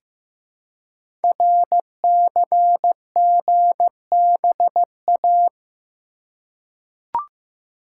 Morse code, heard as a keyed tone, about 15 wpm, 700 Hz. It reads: RCGBA
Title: Code Groups - 5 Letters - 15wpm